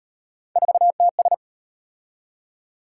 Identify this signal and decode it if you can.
Morse code, heard as a keyed tone, about 38 wpm, 700 Hz. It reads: 4TS